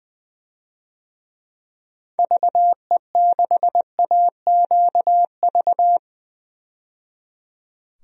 Morse code, heard as a keyed tone, about 20 wpm, 700 Hz. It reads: VE6AQV
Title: DXCC-100 Call Signs - 6 Characters - 20wpm